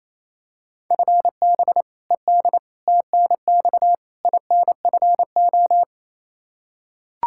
Morse code, heard as a keyed tone, about 28 wpm, 700 Hz. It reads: FB ED TNX INFO
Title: K1USN SST QSO Practice 28wpm